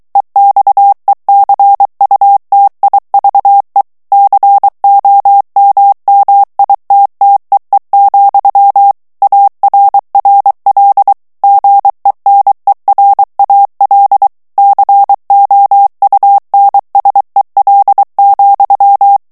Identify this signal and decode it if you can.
Morse code, heard as a keyed tone, about 23 wpm, 800 Hz. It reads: EXECUTIVE COMMITTEE, ARRL GENERAL COUNSEL,
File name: ANGLAIS texte clair 3 vitesse 22 mots minute